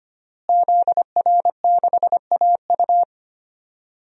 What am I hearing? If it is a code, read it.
Morse code, heard as a keyed tone, about 25 wpm, 700 Hz. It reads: ZR6AU
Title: DXCC-100 Call Signs - 5 Characters - Repeated 3x - 25wpm